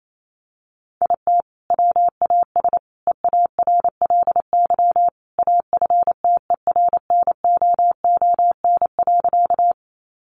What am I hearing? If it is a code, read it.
Morse code, heard as a keyed tone, about 28 wpm, 700 Hz. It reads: IT WAS EARLY AFTERNOON.
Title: Sentences from Top 1500 Words 28wpm